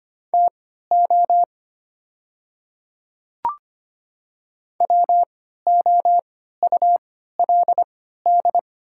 Morse code, heard as a keyed tone, about 25 wpm, spaced as for 14 wpm, 700 Hz. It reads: TO WOULD
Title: 3x Character Spacing - Top 100 Words - 25wpm